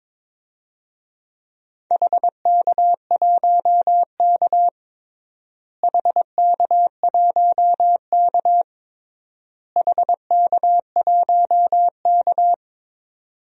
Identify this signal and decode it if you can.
Morse code, heard as a keyed tone, about 22 wpm, 700 Hz. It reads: HK1K HK1K HK1K
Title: DXCC-100 Call Signs - 4 Characters - Repeated 3x - 22wpm